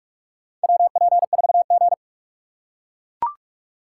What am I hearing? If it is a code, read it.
Morse code, heard as a keyed tone, about 44 wpm, 700 Hz. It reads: WP4G
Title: USA Call Signs - 4 Characters - 6x Character Spacing - 45wpm